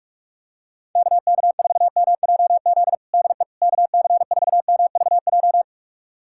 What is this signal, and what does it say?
Morse code, heard as a keyed tone, about 41 wpm, 700 Hz. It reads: KK4MJZ DE KC4MVJ
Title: USA Callsign DE Callsign - 6 Characters - 45wpm